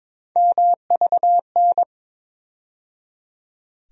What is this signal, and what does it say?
Morse code, heard as a keyed tone, about 22 wpm, 700 Hz. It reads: MVN
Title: USA Call Sign Postfixes - 3 Letters 22wpm